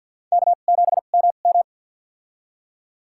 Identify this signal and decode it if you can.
Morse code, heard as a keyed tone, about 47 wpm, spaced as for 35 wpm, 700 Hz. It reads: K7MM
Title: USA Call Signs - 4 Characters - 2x Character Spacing - 50wpm